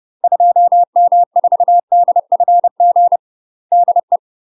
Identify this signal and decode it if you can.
Morse code, heard as a keyed tone, about 30 wpm, 700 Hz. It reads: 2M4DFG DE
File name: UK MASTER QSOs TRAINING / 30WPM / UK MASTER QSOs 0028